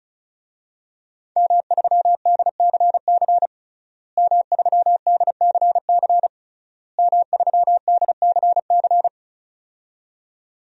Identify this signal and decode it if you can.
Morse code, heard as a keyed tone, about 34 wpm, 690 Hz. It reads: M3DCC M3DCC M3DCC
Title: DXCC-100 Call Signs - 5 Characters - Repeated 3x - 35wpm